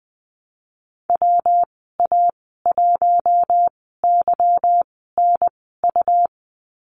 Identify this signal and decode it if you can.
Morse code, heard as a keyed tone, about 20 wpm, spaced as for 14 wpm, 700 Hz. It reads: WA1YNU
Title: 179 Lesson 29g - Slash - Call Signs 20wpm